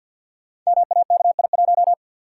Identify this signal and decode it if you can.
Morse code, heard as a keyed tone, about 52 wpm, 700 Hz. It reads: MAKI1
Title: Code Groups - 5 Letter:Numbers - 50wpm